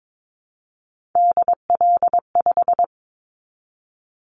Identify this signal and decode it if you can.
Morse code, heard as a keyed tone, about 22 wpm, 700 Hz. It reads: DL5
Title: Callsign Prefixes for Top 50 DXCC - 22wpm